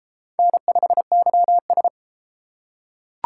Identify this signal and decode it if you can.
Morse code, heard as a keyed tone, about 30 wpm, 700 Hz. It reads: N5YS